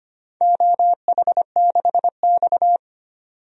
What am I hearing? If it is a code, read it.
Morse code, heard as a keyed tone, about 25 wpm, 700 Hz. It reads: OH6X
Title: DXCC-100 Call Signs - 4 Characters - 25wpm